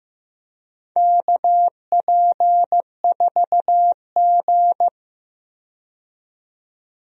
Morse code, heard as a keyed tone, about 15 wpm, 700 Hz. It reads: KP4G
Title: DXCC-100 Call Signs - 4 Characters - 15wpm